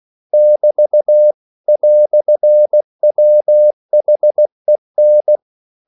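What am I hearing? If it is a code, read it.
Morse code, heard as a keyed tone, about 16 wpm, 600 Hz. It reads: <BT> "WHEN
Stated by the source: English quotations sent in CW as practice at 16 words per minute.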